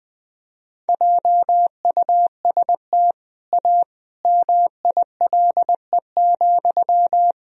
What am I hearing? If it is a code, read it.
Morse code, heard as a keyed tone, about 20 wpm, 700 Hz. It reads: JUST A MILE,
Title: Sentences from Top 1000 Words - Review Part 1 - 20wpm